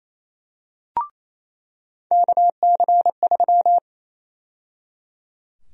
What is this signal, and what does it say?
Morse code, heard as a keyed tone, about 28 wpm, 700 Hz. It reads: KC3
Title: USA Call Sign Prefixes - 28wpm